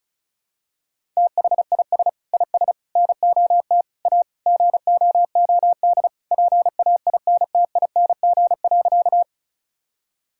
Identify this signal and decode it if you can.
Morse code, heard as a keyed tone, about 34 wpm, 700 Hz. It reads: THIS IS NOT A GOOD PAINTING.
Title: Sentences from Top 1000 Words - Review Part 6 - 35wpm